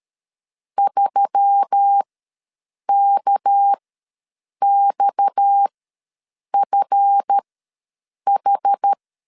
Morse code, heard as a keyed tone, about 13 wpm, spaced as for 7 wpm, 800 Hz. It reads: 3KXFH
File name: LEZIONI / 8 YJ45 / 8 YJ4 ws 12wpm 3249